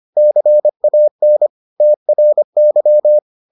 Morse code, heard as a keyed tone, about 25 wpm, 600 Hz. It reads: CAN TRY